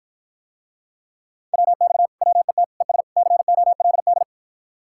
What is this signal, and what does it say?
Morse code, heard as a keyed tone, about 54 wpm, 700 Hz. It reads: WX WET ES COLD